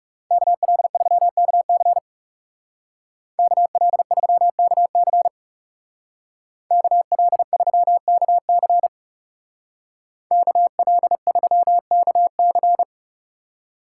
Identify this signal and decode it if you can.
Morse code, heard as a keyed tone, about 36 wpm, 700 Hz. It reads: KL3KC KL3KC KL3KC KL3KC
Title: Speed Racing - USA 2x2 Call Signs 30wpm